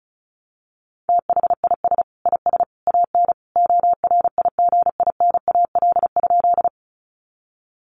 Morse code, heard as a keyed tone, about 34 wpm, 700 Hz. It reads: THIS IS AN ORIGINAL?
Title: Sentences from Top 900 Words 35wpm